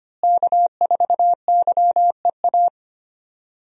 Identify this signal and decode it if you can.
Morse code, heard as a keyed tone, about 25 wpm, 700 Hz. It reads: K4YEA
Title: USA Call Signs - 5 Characters - Repeated 3x - 25wpm